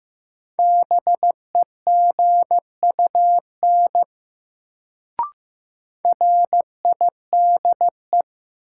Morse code, heard as a keyed tone, about 15 wpm, 700 Hz. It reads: BEGUN RIDE
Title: Sentences from Top 1000 Words - Review Part 6 - New Word Drill - 15wpm